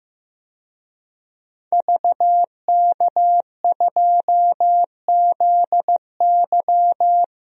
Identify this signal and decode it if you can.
Morse code, heard as a keyed tone, about 16 wpm, 700 Hz. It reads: VK2ZY
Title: DXCC-100 Call Signs - 5 Characters - Repeated 2x - 15wpm